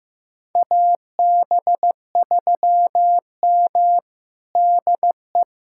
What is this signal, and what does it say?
Morse code, heard as a keyed tone, about 15 wpm, 700 Hz. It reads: AB3M DE
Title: USA Callsign DE Callsign - 4 Characters - 15wpm